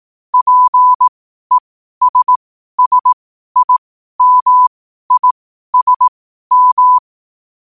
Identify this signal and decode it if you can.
Morse code, heard as a keyed tone, about 18 wpm, spaced as for 12 wpm, 1000 Hz. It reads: PESSIMISM
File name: qotdfiles / 12wpm